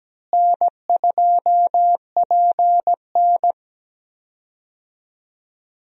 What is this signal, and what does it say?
Morse code, heard as a keyed tone, about 17 wpm, 700 Hz. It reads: N2PN
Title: USA Call Signs - 4 Characters - Repeated 3x - 17wpm